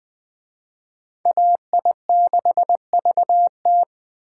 Morse code, heard as a keyed tone, about 20 wpm, 700 Hz. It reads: AI6VT
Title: Speed Racing - USA 2x2 Call Signs 17wpm